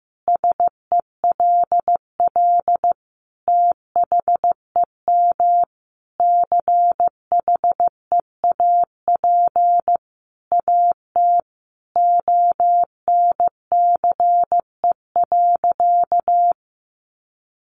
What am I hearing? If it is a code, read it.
Morse code, heard as a keyed tone, about 15 wpm, 700 Hz. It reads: SELL THEM CHEAP AT ONCE.
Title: Sentences from Top 1200 Words 15wpm